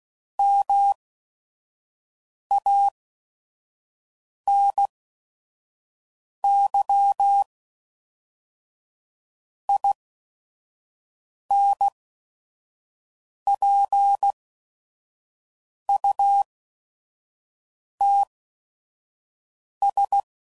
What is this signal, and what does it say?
Morse code, heard as a keyed tone, about 16 wpm, spaced as for 5 wpm, 800 Hz. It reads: MANY INPUTS
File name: CW Code Groups @5WPM / Technical Subjects / AND Gate